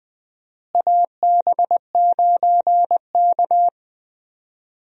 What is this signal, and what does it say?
Morse code, heard as a keyed tone, about 21 wpm, 700 Hz. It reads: AB9K
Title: USA Call Signs - 4 Characters - Repeated 2x - 20wpm